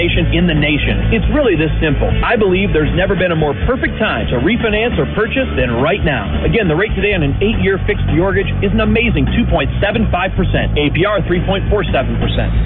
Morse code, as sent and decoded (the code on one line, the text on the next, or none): none